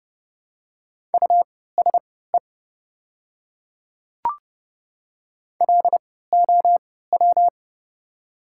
Morse code, heard as a keyed tone, about 30 wpm, spaced as for 17 wpm, 700 Hz. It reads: USE LOW
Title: 3x Character Spacing - Top 300 Words - 30wpm